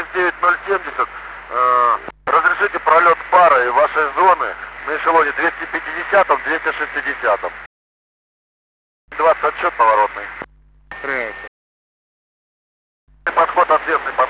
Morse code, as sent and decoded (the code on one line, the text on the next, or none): none